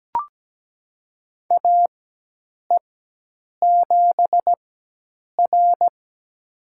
Morse code, heard as a keyed tone, about 17 wpm, spaced as for 8 wpm, 700 Hz. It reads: AE7R
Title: USA Call Signs - 4 Characters - 4x Character Spacing - 17wpm